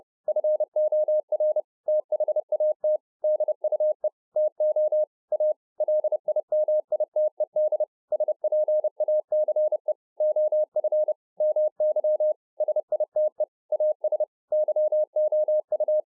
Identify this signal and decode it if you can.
Morse code, heard as a keyed tone, about 30 wpm, 610 Hz. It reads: FOR THAT DUE TO A LIMITED SPACE OF MY SITE AS YOU